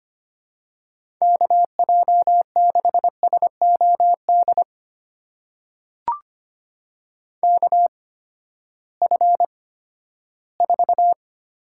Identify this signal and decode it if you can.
Morse code, heard as a keyed tone, about 25 wpm, spaced as for 14 wpm, 700 Hz. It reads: KJ6SOD KF4
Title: USA Call Signs - 6 Characters - 8x Character Spacing - 25wpm